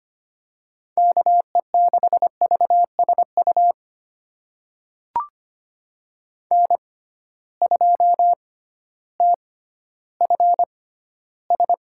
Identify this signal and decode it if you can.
Morse code, heard as a keyed tone, about 25 wpm, spaced as for 14 wpm, 700 Hz. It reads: KE6VSU N2TFS
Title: USA Call Signs - 6x Character Spacing - 25wpm